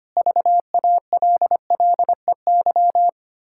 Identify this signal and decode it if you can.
Morse code, heard as a keyed tone, about 25 wpm, 700 Hz. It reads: VALLEY
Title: Pangram Challenge v2 - 25wpm